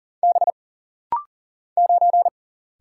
Morse code, heard as a keyed tone, about 40 wpm, 700 Hz. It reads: B 9
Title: Single Letter-Number - Rapid-Fire 40wpm